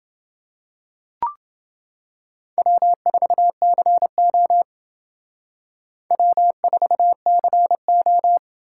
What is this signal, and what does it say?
Morse code, heard as a keyed tone, about 29 wpm, 700 Hz. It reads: W4CO W4CO